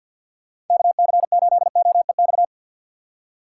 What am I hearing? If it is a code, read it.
Morse code, heard as a keyed tone, about 50 wpm, 700 Hz. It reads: KC8OEX
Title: USA Call Signs - 6 Characters - 3x Character Spacing - 50wpm